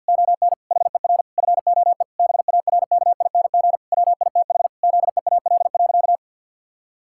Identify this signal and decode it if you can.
Morse code, heard as a keyed tone, about 49 wpm, 700 Hz. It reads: ON HER FOE BARKING WITH ZEAL.
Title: Pangram Challenge v2 - 50wpm